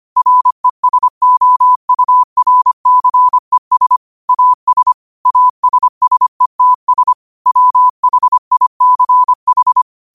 Morse code, heard as a keyed tone, about 25 wpm, 1000 Hz. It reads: RESOURCES AS ASSETS WHICH